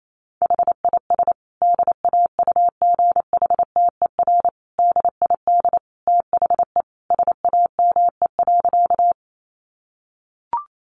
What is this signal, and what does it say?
Morse code, heard as a keyed tone, about 28 wpm, 700 Hz. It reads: HIS DAUGHTER DID THE SAME.
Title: Sentences from Top 1500 Words 28wpm